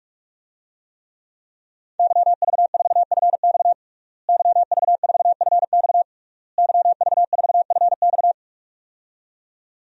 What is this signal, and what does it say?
Morse code, heard as a keyed tone, about 47 wpm, 700 Hz. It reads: YV4FX YV4FX YV4FX
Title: DXCC-100 Call Signs - 5 Characters - Repeated 3x - 45wpm